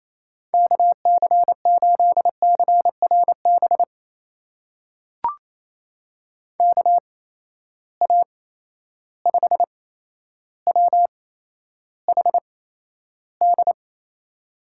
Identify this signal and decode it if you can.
Morse code, heard as a keyed tone, about 28 wpm, spaced as for 12 wpm, 700 Hz. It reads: KC8CRB KA5WHD
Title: USA Call Signs - 6 Characters - 8x Character Spacing - 28wpm